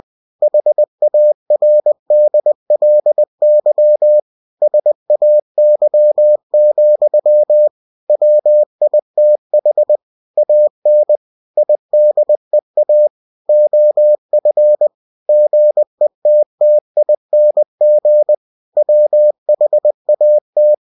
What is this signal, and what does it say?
Morse code, heard as a keyed tone, about 20 wpm, 600 Hz. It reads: HARDLY SAY, WITH AN IDEA OF GETTING WHAT
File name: cw-books / winnie-the-pooh / winnie-20wpm-page-0107